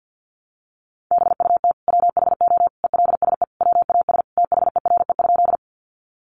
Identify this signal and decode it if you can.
Morse code, heard as a keyed tone, about 50 wpm, 700 Hz. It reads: BUT WHO ELSE WAS THERE?
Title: Sentences from Top 800 Words 50wpm